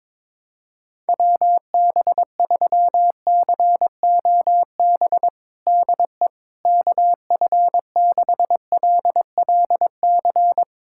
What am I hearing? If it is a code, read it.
Morse code, heard as a keyed tone, about 22 wpm, 700 Hz. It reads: WB3COB DE KF6LLC
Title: USA Callsign DE Callsign - 6 Characters - 22wpm